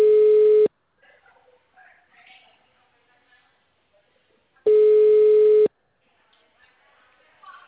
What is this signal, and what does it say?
Ringback tone (Europe, 425 Hz), with speech.